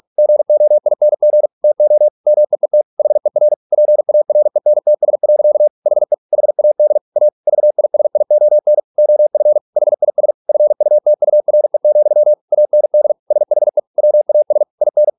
Morse code, heard as a keyed tone, about 46 wpm, 600 Hz. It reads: GOING TO MEET HER PARENTS. HE HAD A VISION OF HIS FUTURE, AND SHE WAS IN